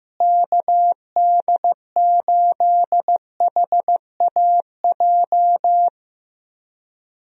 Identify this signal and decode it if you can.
Morse code, heard as a keyed tone, about 15 wpm, 700 Hz. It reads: KD8HAJ
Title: USA Call Signs - 6 Characters - Repeated 3x - 15wpm